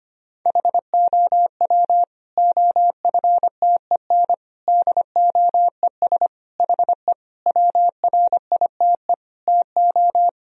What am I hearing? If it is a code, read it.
Morse code, heard as a keyed tone, about 25 wpm, 700 Hz. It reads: HOW OFTEN DOES HE WRITE TO
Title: Sentences from Top 1000 Words - Review Part 1 - 25wpm